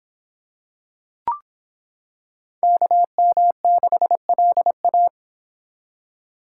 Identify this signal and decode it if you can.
Morse code, heard as a keyed tone, about 25 wpm, 700 Hz. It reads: KM6LA